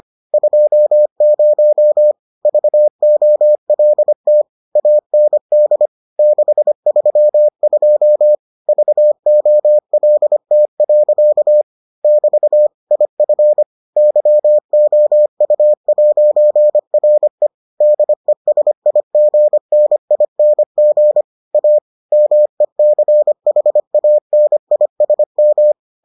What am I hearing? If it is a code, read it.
Morse code, heard as a keyed tone, about 25 wpm, 600 Hz. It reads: 20 VOLT AND 632 VOLT. <BT> IF YOU'RE DESIGNING A MECHANISM